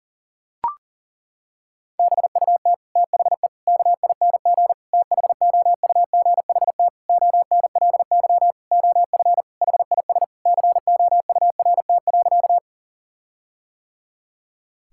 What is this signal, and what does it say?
Morse code, heard as a keyed tone, about 37 wpm, 700 Hz. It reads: BUT THE KING THOUGHT ONLY OF HIS COURT.
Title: Sentences from Top 1600 Words 40wpm